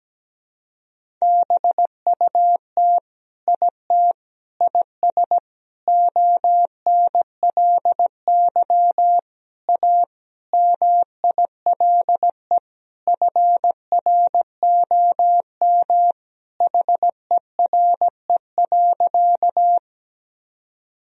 Morse code, heard as a keyed tone, about 17 wpm, 700 Hz. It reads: BUT IT IS ONLY A MILE FROM HERE.
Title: Sentences from Top 1000 Words - Review Part 1 - 17wpm